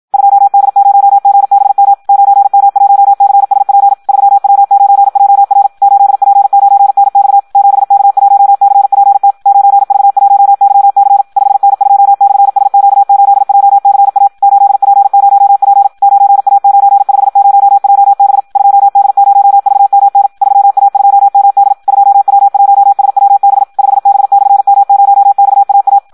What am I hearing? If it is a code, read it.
Morse code, heard as a keyed tone, about 56 wpm, 800 Hz. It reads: 2D0GBM 9M1CSP 3W9JR 8P9NX 7K1CPT 9V1YC 5N2/SQ8JCA 8P0P 9A950JB 2D0VMN 3A2MD 2W1SWB 5B/M0XAA
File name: Audio practica CW / qrq-tx-54 2